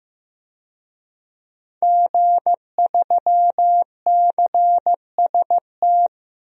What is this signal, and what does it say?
Morse code, heard as a keyed tone, about 16 wpm, 700 Hz. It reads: G3CST
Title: DXCC-100 Call Signs - 5 Characters - Repeated 3x - 15wpm